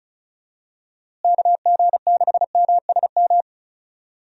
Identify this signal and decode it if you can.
Morse code, heard as a keyed tone, about 34 wpm, 700 Hz. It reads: KG6MSM
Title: USA Call Signs - 6 Characters - Repeated 3x - 35wpm